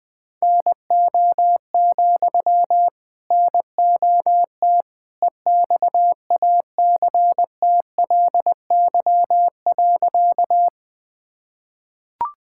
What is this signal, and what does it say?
Morse code, heard as a keyed tone, about 21 wpm, 700 Hz. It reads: NO, NOT EXACTLY.
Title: Sentences from Top 1000 Words - Review Part 6 - 20wpm